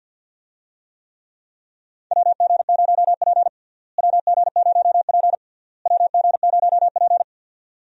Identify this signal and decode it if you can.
Morse code, heard as a keyed tone, about 48 wpm, 700 Hz. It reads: WG0P WG0P WG0P
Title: USA Call Signs - 4 Characters - Repeated 3x - 50wpm